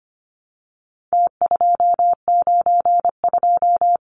Morse code, heard as a keyed tone, about 25 wpm, 700 Hz. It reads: T292
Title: Numbers - Contest Style - 4-Digit 25wpm